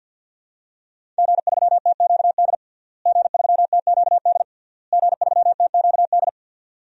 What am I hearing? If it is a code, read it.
Morse code, heard as a keyed tone, about 49 wpm, 700 Hz. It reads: G3TQD G3TQD G3TQD
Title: DXCC-100 Call Signs - 5 Characters - Repeated 3x - 50wpm